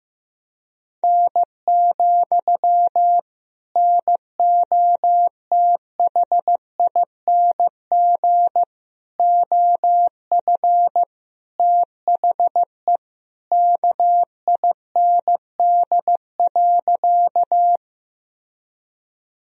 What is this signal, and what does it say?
Morse code, heard as a keyed tone, about 15 wpm, 700 Hz. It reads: N, NOTHING OF THE KIND.
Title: Sentences from Top 500 Words and Abbreviations 15wpm